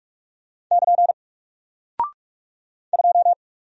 Morse code, heard as a keyed tone, about 44 wpm, 700 Hz. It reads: <KN> 2
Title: Single Character (Letter:Number:Punctuation:Prosign) 45wpm